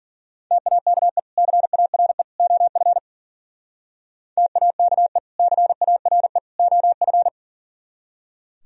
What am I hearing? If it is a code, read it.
Morse code, heard as a keyed tone, about 50 wpm, 700 Hz. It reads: TAKE CARE OF TAKE CARE OF